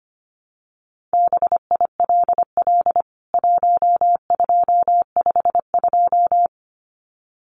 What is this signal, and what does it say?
Morse code, heard as a keyed tone, about 25 wpm, 700 Hz. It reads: BILL 1252
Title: CWT Exchange 25wpm